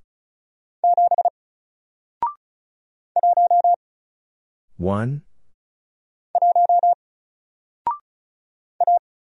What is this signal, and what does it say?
Morse code, heard as a keyed tone, about 36 wpm, 700 Hz. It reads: 7 1 1 A
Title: Single Character (Letter:Number:Punctuation:Prosign) 35wpm